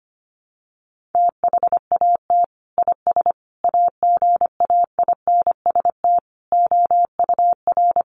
Morse code, heard as a keyed tone, about 25 wpm, 700 Hz. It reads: THAT IS AGAINST OUR
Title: Sentences from Top 1500 Words 25wpm